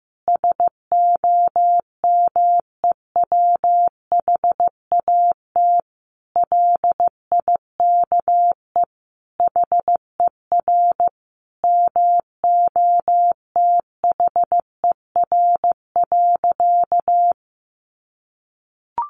Morse code, heard as a keyed tone, about 15 wpm, 700 Hz. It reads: SOMEWHAT LIKE HER MOTHER.
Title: Sentences from Top 1300 Words 15wpm